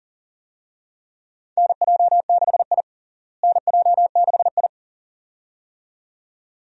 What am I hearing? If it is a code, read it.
Morse code, heard as a keyed tone, about 40 wpm, 700 Hz. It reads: NJ6I NJ6I
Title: USA Call Signs - 4 Characters - Repeated 2x - 40wpm